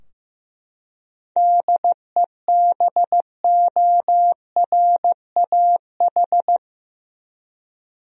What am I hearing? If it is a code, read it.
Morse code, heard as a keyed tone, about 15 wpm, 700 Hz. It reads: DEBORAH
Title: USA Common Names - 15wpm